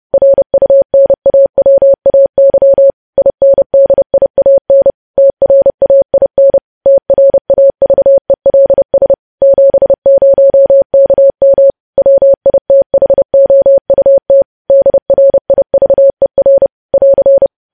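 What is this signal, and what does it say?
Morse code, heard as a keyed tone, about 30 wpm, 550 Hz. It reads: RUNAWAY INDIAN TRAIN TRAVELS 70KM WITHOUT DRIVER <AR>
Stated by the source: News headlines sent in Morse at 30 words per minute.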